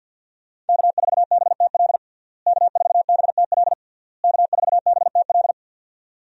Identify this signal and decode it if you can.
Morse code, heard as a keyed tone, about 50 wpm, 700 Hz. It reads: K4BTL K4BTL K4BTL
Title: USA Call Signs - 5 Characters - Repeated 3x - 50wpm